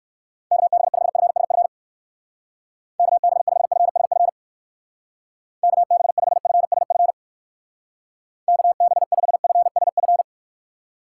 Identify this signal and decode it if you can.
Morse code, heard as a keyed tone, about 59 wpm, 700 Hz. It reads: KB5FSF KB5FSF KB5FSF KB5FSF